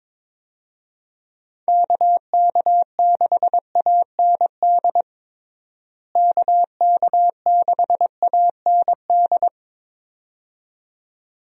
Morse code, heard as a keyed tone, about 22 wpm, 700 Hz. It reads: KK6AND KK6AND
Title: USA Call Signs - 6 Characters - Repeated 2x - 22wpm